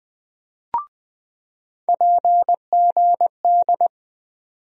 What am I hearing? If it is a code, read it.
Morse code, heard as a keyed tone, about 20 wpm, 700 Hz. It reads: PGD
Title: USA Call Sign Postfixes - 3 Letters 20wpm